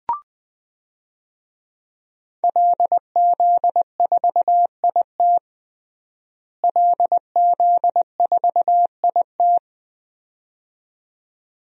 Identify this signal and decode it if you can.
Morse code, heard as a keyed tone, about 20 wpm, 700 Hz. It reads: LZ4IT LZ4IT
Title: DXCC-100 Call Signs - 5 Characters - Repeated 2x - 20wpm